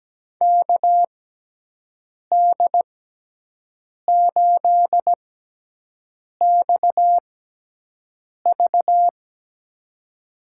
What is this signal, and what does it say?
Morse code, heard as a keyed tone, about 17 wpm, spaced as for 6 wpm, 700 Hz. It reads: KD8XV
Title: USA Call Signs - 6x Character Spacing - 17wpm